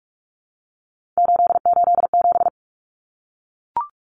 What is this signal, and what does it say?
Morse code, heard as a keyed tone, about 44 wpm, 700 Hz. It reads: 887
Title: Common Cut Numbers - 3-Digit 45wpm